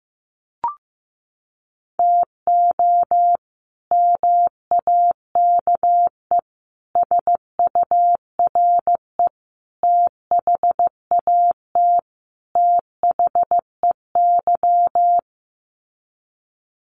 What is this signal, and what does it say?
Morse code, heard as a keyed tone, about 15 wpm, 700 Hz. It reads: TO MAKE SURE THAT THEY
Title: Sets of 5 Words Encore2 15wpm